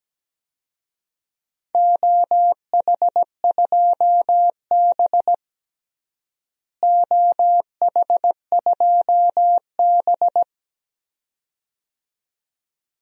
Morse code, heard as a keyed tone, about 17 wpm, 700 Hz. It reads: OH2B OH2B
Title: DXCC-100 Call Signs - 4 Characters - Repeated 2x - 17wpm